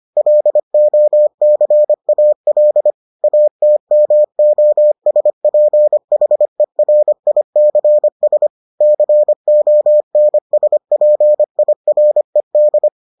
Morse code, heard as a keyed tone, about 25 wpm, 600 Hz. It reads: LOCAL ATMOSPHERICS CONSPIRED